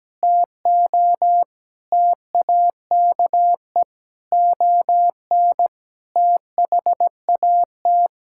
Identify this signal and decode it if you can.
Morse code, heard as a keyed tone, about 17 wpm, 700 Hz. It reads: TO TAKE ON THAT
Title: Sentences from Top 1000 Words - Review Part 6 - 17wpm